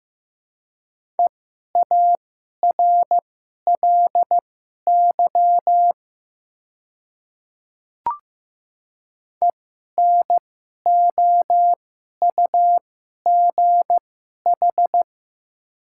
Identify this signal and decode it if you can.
Morse code, heard as a keyed tone, about 15 wpm, spaced as for 11 wpm, 700 Hz. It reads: EARLY ENOUGH
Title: 2x Character Spacing - Top 500 Words - 15wpm